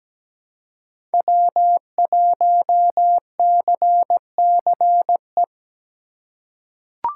Morse code, heard as a keyed tone, about 17 wpm, 700 Hz. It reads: W1CCE
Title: K1USN SST QSO Practice - 17wpm